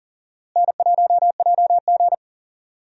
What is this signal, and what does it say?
Morse code, heard as a keyed tone, about 40 wpm, 700 Hz. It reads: N1JG
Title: USA Call Signs - 4 Characters - Repeated 3x - 40wpm